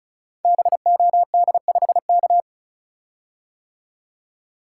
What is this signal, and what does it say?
Morse code, heard as a keyed tone, about 34 wpm, 700 Hz. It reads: BOD5K
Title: Code Groups - 5 Letter:Numbers - 35wpm